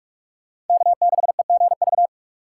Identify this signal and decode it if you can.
Morse code, heard as a keyed tone, about 42 wpm, 700 Hz. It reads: K6EGV